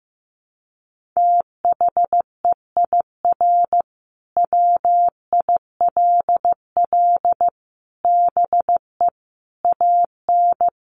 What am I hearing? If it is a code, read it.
Morse code, heard as a keyed tone, about 15 wpm, 700 Hz. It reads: THEIR WILL BE AN
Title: Sentences from Top 1500 Words 15wpm